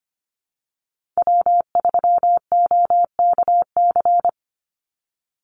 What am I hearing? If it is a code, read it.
Morse code, heard as a keyed tone, about 25 wpm, 700 Hz. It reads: W3OKC
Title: Call Signs - 25wpm